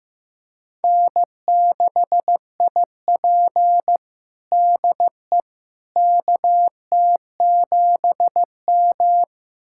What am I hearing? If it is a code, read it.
Morse code, heard as a keyed tone, about 15 wpm, 700 Hz. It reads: N6IP DE KT7M
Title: USA Callsign DE Callsign - 4 Characters - 15wpm